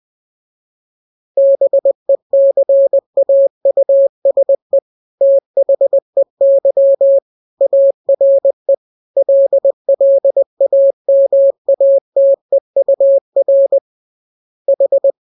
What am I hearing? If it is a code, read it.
Morse code, heard as a keyed tone, about 20 wpm, 550 Hz. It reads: BECAUSE THEY ARE LLAMATEUR H
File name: jokes / joke 14 20 0000